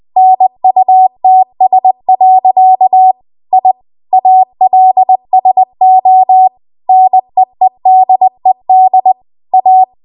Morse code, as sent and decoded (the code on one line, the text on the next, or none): -. ..- - ... .-.-.- / .. / .- .-.. ... --- / -. . -.. -.. / .-
NUTS. I ALSO NEEDED A